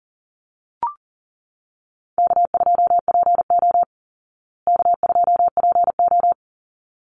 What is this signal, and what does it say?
Morse code, heard as a keyed tone, about 40 wpm, 700 Hz. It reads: K2PO K2PO
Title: Easy Call Signs Repeated 2x - 40wpm